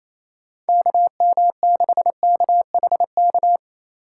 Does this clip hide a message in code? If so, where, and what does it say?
Yes: Morse code from 0.7 to 3.6 s, heard as a keyed tone, about 37 wpm, 700 Hz. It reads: KM6KHK